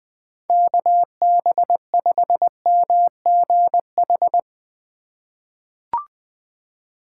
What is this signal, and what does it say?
Morse code, heard as a keyed tone, about 21 wpm, 700 Hz. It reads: KB5MGH
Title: USA Call Signs - 6 Characters - 4x Character Spacing - 20wpm